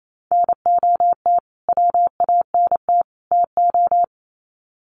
Morse code, heard as a keyed tone, about 28 wpm, 700 Hz. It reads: NOT WANT TO
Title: Sets of 3 Words 28wpm